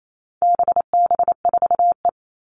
Morse code, heard as a keyed tone, about 28 wpm, 700 Hz. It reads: BB4E
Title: All Cut Numbers - 4-Digit 28wpm